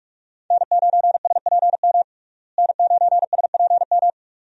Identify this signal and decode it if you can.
Morse code, heard as a keyed tone, about 42 wpm, 700 Hz. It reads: N9SPM N9SPM
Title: USA Call Signs - 5 Characters - Repeated 2x - 45wpm